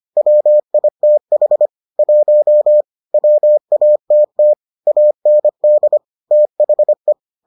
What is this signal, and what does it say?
Morse code, heard as a keyed tone, about 25 wpm, 600 Hz. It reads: WITH 1 WATT AND THE